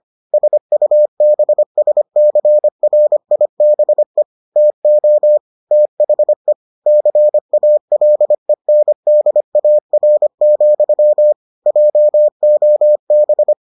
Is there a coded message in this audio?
Yes: Morse code from 0.3 to 13.5 s, heard as a keyed tone, about 25 wpm, 600 Hz. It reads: SUBSCRIBE TO THE CALENDAR, JOB